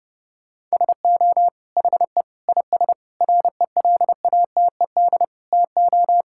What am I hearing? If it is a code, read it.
Morse code, heard as a keyed tone, about 30 wpm, 700 Hz. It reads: SO HE IS RELATED TO
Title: Sentences from Top 1000 Words - Review Part 5 - 30wpm